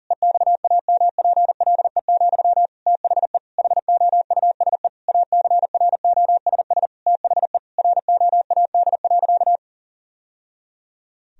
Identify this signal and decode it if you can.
Morse code, heard as a keyed tone, about 40 wpm, 700 Hz. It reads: EXAMPLE, THE HOUSE ACROSS THE ROAD.
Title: Sentences from Top 1000 Words - Review Part 1 - 40wpm